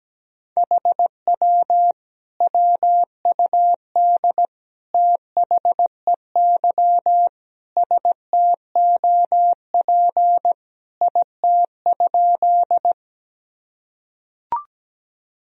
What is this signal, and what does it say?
Morse code, heard as a keyed tone, about 17 wpm, 700 Hz. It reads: HW WUD THEY STOP IT?
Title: Sentences from Top 300 Words and Abbreviations 17wpm